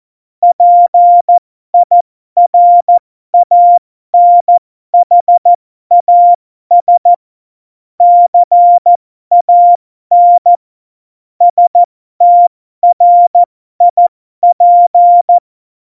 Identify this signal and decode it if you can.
Morse code, heard as a keyed tone, about 14 wpm, spaced as for 12 wpm, 700 Hz. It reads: PIRANHAS CAN STRIP
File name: Morsekurs / Witze